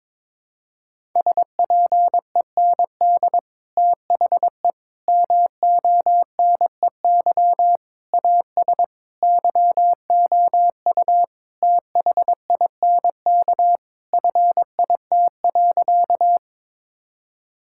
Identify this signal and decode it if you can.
Morse code, heard as a keyed tone, about 22 wpm, 700 Hz. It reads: SPEND THE MONEY AS YOU THINK FIT.
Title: Sentences from Top 1000 Words - Review Part 5 - 22wpm